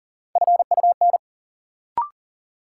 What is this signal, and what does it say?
Morse code, heard as a keyed tone, about 40 wpm, 700 Hz. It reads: FUN
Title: Sentences from Top 700 Words - New Word Drill - 40wpm